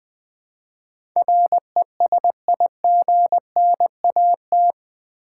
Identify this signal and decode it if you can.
Morse code, heard as a keyed tone, about 20 wpm, 700 Hz. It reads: RESIGNAT
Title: Build A Word - 4 Letters At A Time - 11-Letter Words - 20wpm